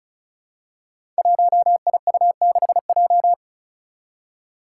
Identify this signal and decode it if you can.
Morse code, heard as a keyed tone, about 36 wpm, 700 Hz. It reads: 1IU6J
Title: Five Challenging Letter-Number Combinations 35wpm